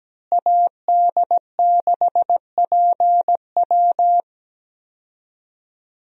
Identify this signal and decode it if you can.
Morse code, heard as a keyed tone, about 17 wpm, 700 Hz. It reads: AD6PW
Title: USA Call Signs - 5 Characters - Repeated 2x - 17wpm